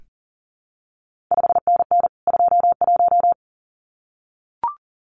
Morse code, heard as a keyed tone, about 40 wpm, 700 Hz. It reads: 5NN 21
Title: CQ WW DX Contest Exchange Encore - 40wpm